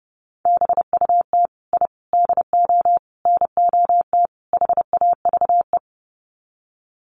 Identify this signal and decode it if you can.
Morse code, heard as a keyed tone, about 30 wpm, 700 Hz. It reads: BUT I DO NOT HAVE
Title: Sets of 5 Words Encore 30wpm